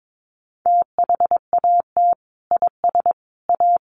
Morse code, heard as a keyed tone, about 22 wpm, 700 Hz. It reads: THAT IS A
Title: Sentences from Top 100 Words 22wpm